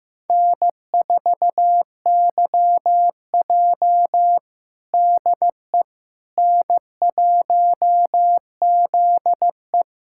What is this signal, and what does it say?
Morse code, heard as a keyed tone, about 15 wpm, 700 Hz. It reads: N4YJ DE N1ZE
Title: USA Callsign DE Callsign - 4 Characters - 15wpm